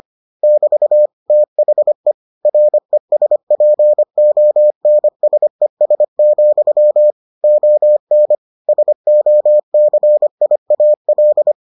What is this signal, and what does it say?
Morse code, heard as a keyed tone, about 25 wpm, 600 Hz. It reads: <BT> THE RESPONSES, ON SOCIAL